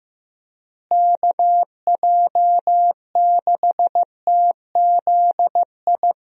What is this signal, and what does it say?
Morse code, heard as a keyed tone, about 15 wpm, 700 Hz. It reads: KJ6TZI
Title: USA Call Signs - 6 Characters - Repeated 2x - 15wpm